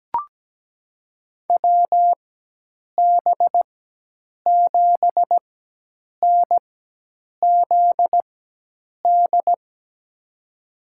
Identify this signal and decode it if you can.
Morse code, heard as a keyed tone, about 17 wpm, spaced as for 8 wpm, 700 Hz. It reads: WB7NZD
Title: USA Call Signs - 6 Characters - 4x Character Spacing - 17wpm